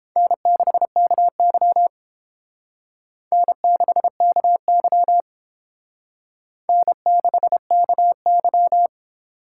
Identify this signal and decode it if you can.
Morse code, heard as a keyed tone, about 28 wpm, 700 Hz. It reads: N6KY N6KY N6KY